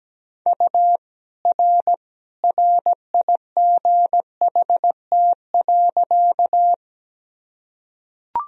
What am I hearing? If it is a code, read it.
Morse code, heard as a keyed tone, about 17 wpm, 700 Hz. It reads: U R RIGHT.
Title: Sentences from Top 200 Words and Abbreviations 17wpm